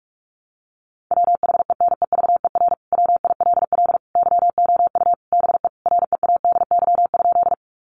Morse code, heard as a keyed tone, about 44 wpm, 700 Hz. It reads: WHENEVER WILL YOU BE READY?
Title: Sentences from Top 1200 Words 45wpm